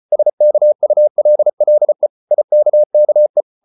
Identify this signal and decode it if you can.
Morse code, heard as a keyed tone, about 34 wpm, 600 Hz. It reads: SKULLE IKKE
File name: Ferie QRQ 34wpm